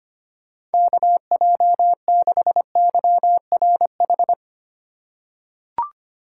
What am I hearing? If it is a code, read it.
Morse code, heard as a keyed tone, about 25 wpm, 700 Hz. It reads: KJ6YRH
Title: USA Call Signs - 6 Characters - 4x Character Spacing - 25wpm